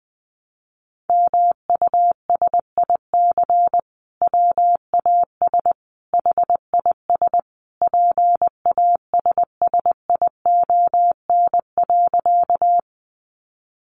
Morse code, heard as a keyed tone, about 20 wpm, 700 Hz. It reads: MUSIC WAS HIS PASSION.
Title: Sentences from Top 1300 Words 20wpm